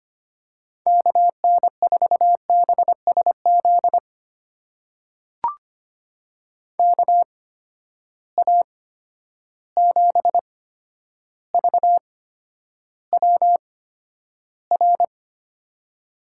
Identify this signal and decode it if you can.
Morse code, heard as a keyed tone, about 25 wpm, spaced as for 11 wpm, 700 Hz. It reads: KN4BSZ KA7VWR
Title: USA Call Signs - 6 Characters - 8x Character Spacing - 25wpm